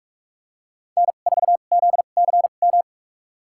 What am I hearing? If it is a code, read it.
Morse code, heard as a keyed tone, about 46 wpm, spaced as for 32 wpm, 700 Hz. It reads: N4ZCM
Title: USA Call Signs - 2x Character Spacing - 45wpm